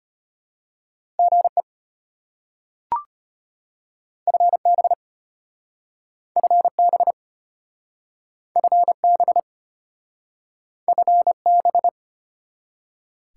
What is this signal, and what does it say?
Morse code, heard as a keyed tone, about 34 wpm, 690 Hz. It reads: GE FB FB FB FB